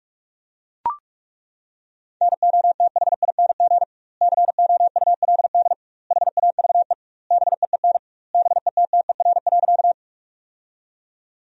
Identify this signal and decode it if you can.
Morse code, heard as a keyed tone, about 41 wpm, 700 Hz. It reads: NOTHING COULD HAVE BEEN BETTER.
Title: 2x Word-Spacing - Sentences from Top 500 Words - 45wpm